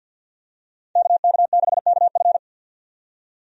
Morse code, heard as a keyed tone, about 52 wpm, 700 Hz. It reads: KK6KF
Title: USA Call Signs - 5 Characters - 50wpm